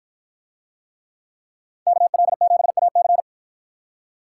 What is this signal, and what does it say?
Morse code, heard as a keyed tone, about 53 wpm, 700 Hz. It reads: KL7AC